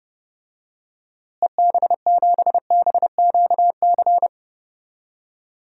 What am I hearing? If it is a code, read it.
Morse code, heard as a keyed tone, about 30 wpm, 700 Hz. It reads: EB7BQC
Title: DXCC-100 Call Signs - 6 Characters - Repeated 2x - 30wpm